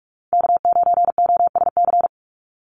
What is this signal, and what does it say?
Morse code, heard as a keyed tone, about 46 wpm, 700 Hz. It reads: K9OSC
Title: NA SOTA Chaser Callsigns 45wpm